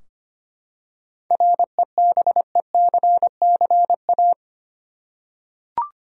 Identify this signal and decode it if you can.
Morse code, heard as a keyed tone, about 25 wpm, 700 Hz. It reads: REBECCA
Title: USA Common Names - 25wpm